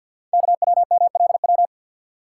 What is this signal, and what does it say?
Morse code, heard as a keyed tone, about 48 wpm, 700 Hz. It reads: KWMLW
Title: Code Groups - 5 Letters - 50wpm